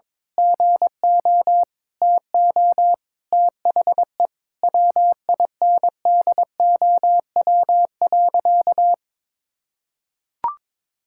Morse code, heard as a keyed tone, about 22 wpm, 700 Hz. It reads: GO TO THE WINDOW.
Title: Sentences from Top 1600 Words 22wpm